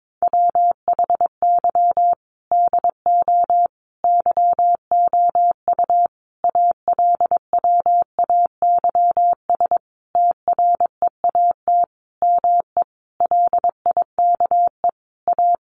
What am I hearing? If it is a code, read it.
Morse code, heard as a keyed tone, about 22 wpm, 700 Hz. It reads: WHY DO YOU ALWAYS TREAT ME LIKE A
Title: Sentences from Top 1300 Words 22wpm